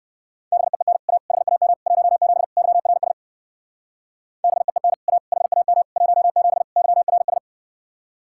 Morse code, heard as a keyed tone, about 59 wpm, 700 Hz. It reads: BEEN A HAM 17 YRS BEEN A HAM 17 YRS